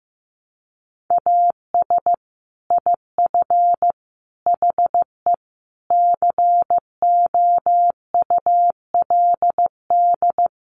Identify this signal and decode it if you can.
Morse code, heard as a keyed tone, about 15 wpm, 700 Hz. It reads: AS IF HE COULD
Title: Sentences from Top 1300 Words 15wpm